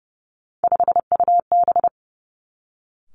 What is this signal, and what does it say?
Morse code, heard as a keyed tone, about 30 wpm, 700 Hz. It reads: 5UB
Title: Three Challenging Letter-Number Combinations 30wpm